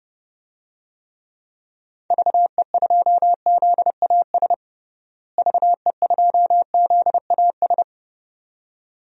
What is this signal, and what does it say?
Morse code, heard as a keyed tone, about 29 wpm, 700 Hz. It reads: VE2ZAS VE2ZAS
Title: DXCC-100 Call Signs - 6 Characters - Repeated 2x - 30wpm